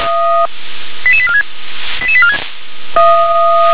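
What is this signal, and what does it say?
Five-tone selective call (ZVEI), heard as quick runs of five tones, tones of about 70 ms. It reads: 88846 88846
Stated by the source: ZVEI five-tone selective call received on SDR.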